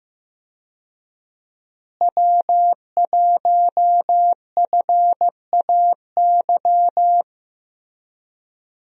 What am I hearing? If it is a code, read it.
Morse code, heard as a keyed tone, about 15 wpm, 700 Hz. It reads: W1FAY
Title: USA Call Signs - 5 Characters - Repeated 2x - 15wpm